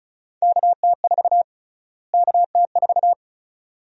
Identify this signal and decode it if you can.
Morse code, heard as a keyed tone, about 36 wpm, 700 Hz. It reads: KT4 KT4
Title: USA Call Sign Prefixes - Repeated 2x - 35wpm